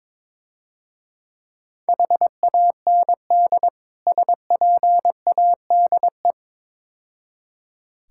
Morse code, heard as a keyed tone, about 24 wpm, 700 Hz. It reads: HAND SPADE
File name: Easy Words - Speed-Racing with Hints - 22wpm